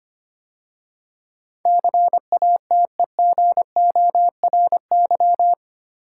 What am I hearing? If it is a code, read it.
Morse code, heard as a keyed tone, about 25 wpm, 690 Hz. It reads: CATEGORY
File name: 8 Letter Words - First 3 Letters Given - 25wpm